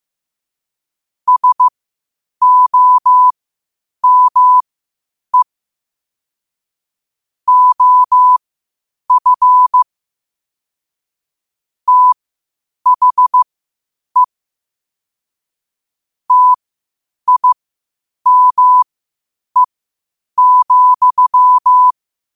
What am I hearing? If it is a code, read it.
Morse code, heard as a keyed tone, about 15 wpm, spaced as for 8 wpm, 1000 Hz. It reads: SOME OF THE TIME,